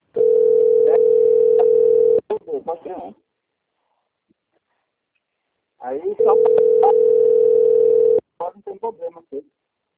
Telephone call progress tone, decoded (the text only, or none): ringback tone